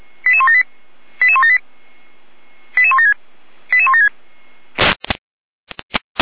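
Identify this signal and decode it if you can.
Five-tone selective call (ZVEI), heard as quick runs of five tones, tones of about 70 ms. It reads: 88178 88178 88176 88176
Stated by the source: ZVEI five-tone selective call 88012, received by SDR, logged on the Feuerwehr Orlamünde fire brigade server.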